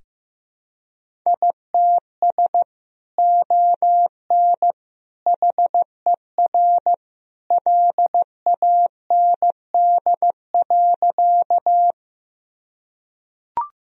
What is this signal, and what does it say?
Morse code, heard as a keyed tone, about 15 wpm, 700 Hz. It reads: ITS ON HER LAND.
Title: Sentences from Top 1000 Words - Review Part 1 - 15wpm